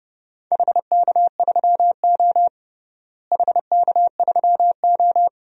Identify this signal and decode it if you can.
Morse code, heard as a keyed tone, about 30 wpm, 700 Hz. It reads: HK3O HK3O
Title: DXCC-100 Call Signs - 4 Characters - Repeated 2x - 30wpm